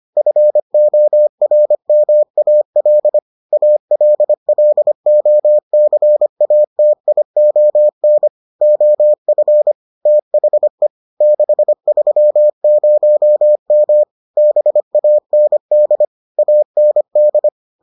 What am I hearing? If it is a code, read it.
Morse code, heard as a keyed tone, about 25 wpm, 600 Hz. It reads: FORMAL ALLOCATION OF THE 630M BAND AND